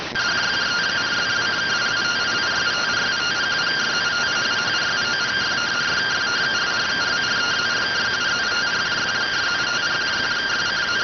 Two-tone teleprinter signal, about 170 Hz shift.